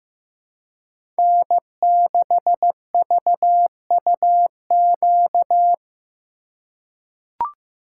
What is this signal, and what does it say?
Morse code, heard as a keyed tone, about 15 wpm, 700 Hz. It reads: N6VUQ
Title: USA Call Signs - 5 Characters - 3x Character Spacing - 15wpm